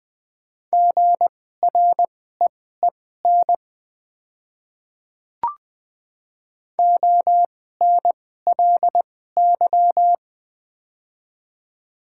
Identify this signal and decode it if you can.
Morse code, heard as a keyed tone, about 20 wpm, spaced as for 14 wpm, 700 Hz. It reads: GREEN ONLY
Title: 2x Character Spacing - Top 500 Words - 20wpm